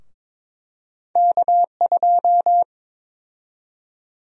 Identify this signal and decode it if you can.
Morse code, heard as a keyed tone, about 22 wpm, 700 Hz. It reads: K2
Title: USA Call Sign Prefixes - Repeated 2x - 22wpm